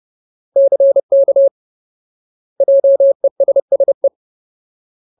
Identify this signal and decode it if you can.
Morse code, heard as a keyed tone, about 30 wpm, 550 Hz. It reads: CK JESSE
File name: Ultimate Word List / 30WPM Ultimate